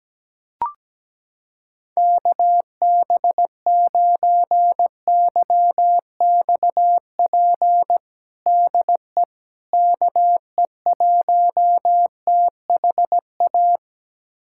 Morse code, heard as a keyed tone, about 17 wpm, 700 Hz. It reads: KB9YXP DE KE1THA
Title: USA Callsign DE Callsign 17wpm